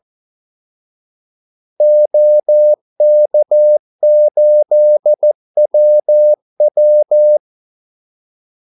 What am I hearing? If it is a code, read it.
Morse code, heard as a keyed tone, about 14 wpm, 600 Hz. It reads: OK8WW
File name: Callsigns-14wpm